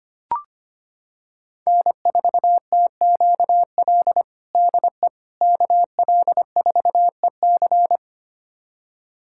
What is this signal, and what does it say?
Morse code, heard as a keyed tone, about 25 wpm, 700 Hz. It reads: N4TQL DE KL4EC
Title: USA Callsign DE Callsign - 5 Characters - 25wpm